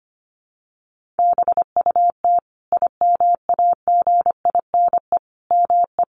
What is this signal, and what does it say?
Morse code, heard as a keyed tone, about 25 wpm, 700 Hz. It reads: BUT IMAGINE ME
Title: Sentences from Top 1400 Words 25wpm